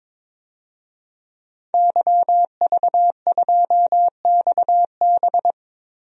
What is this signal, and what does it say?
Morse code, heard as a keyed tone, about 21 wpm, 700 Hz. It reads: YV2XB
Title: DXCC-100 Call Signs - 5 Characters - Repeated 2x - 22wpm